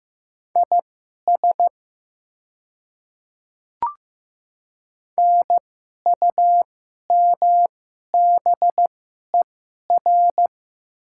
Morse code, heard as a keyed tone, about 15 wpm, spaced as for 11 wpm, 700 Hz. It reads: IS NUMBER
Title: 2x Character Spacing - Top 100 Words - 15wpm